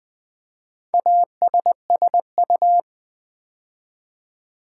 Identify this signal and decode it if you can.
Morse code, heard as a keyed tone, about 20 wpm, 690 Hz. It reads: ASSU
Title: Build A Word - 2 Letters At A Time - 6-Letter Words - 20wpm